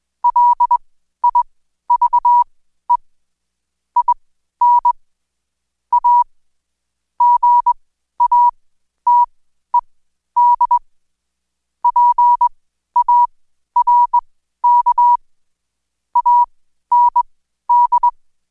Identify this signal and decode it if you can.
Morse code, heard as a keyed tone, about 21 wpm, spaced as for 12 wpm, 970 Hz. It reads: LIVE IN A GATED PARK AND
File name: Code Quick / AudioDownld2 / Seg121-Day26-DayDrill 26